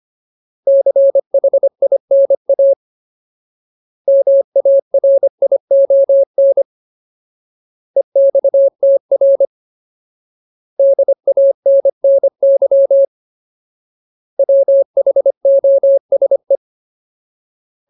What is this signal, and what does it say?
Morse code, heard as a keyed tone, about 25 wpm, 550 Hz. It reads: CHINA MARION EXTR DANNY WHOSE
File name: Ultimate Word List / 25WPM Ultimate